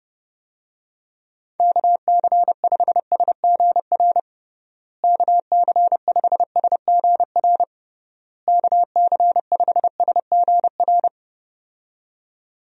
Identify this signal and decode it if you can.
Morse code, heard as a keyed tone, about 32 wpm, 700 Hz. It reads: KC5SGR KC5SGR KC5SGR
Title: USA Call Signs - 6 Characters - Repeated 3x - 30wpm